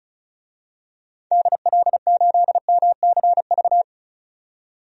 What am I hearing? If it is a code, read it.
Morse code, heard as a keyed tone, about 34 wpm, 700 Hz. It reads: DL8MCV
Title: DXCC-100 Call Signs - 6 Characters - 35wpm